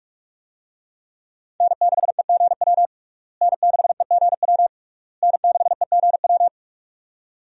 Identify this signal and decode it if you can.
Morse code, heard as a keyed tone, about 46 wpm, 700 Hz. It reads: N6EGW N6EGW N6EGW
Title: USA Call Signs - 5 Characters - Repeated 3x - 45wpm